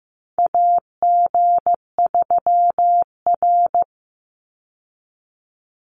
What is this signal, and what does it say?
Morse code, heard as a keyed tone, about 15 wpm, 700 Hz. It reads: AG3R
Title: Call Signs Repeated 3x - 15wpm